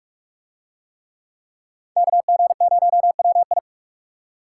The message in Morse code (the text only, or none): KG0WI